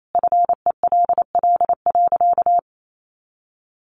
Morse code, heard as a keyed tone, about 28 wpm, 700 Hz. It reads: FELL.
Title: A Sound Of Thunder 28wpm